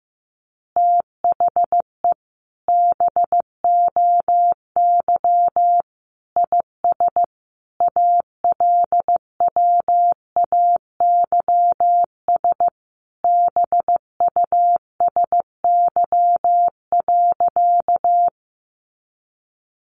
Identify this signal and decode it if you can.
Morse code, heard as a keyed tone, about 14 wpm, 700 Hz. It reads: THE BOY IS ALWAYS BUSY.
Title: Additional Easy Phrases 15wpm